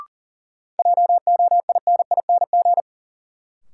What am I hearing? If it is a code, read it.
Morse code, heard as a keyed tone, about 40 wpm, 700 Hz. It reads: JOINING
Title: Sets of 5 Words - New Word Drill - 40wpm